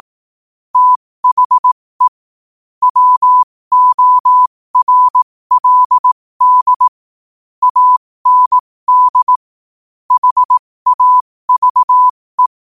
Morse code, heard as a keyed tone, about 18 wpm, spaced as for 15 wpm, 1000 Hz. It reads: THE WORLD AND HAVE
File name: qotdfiles / 15wpm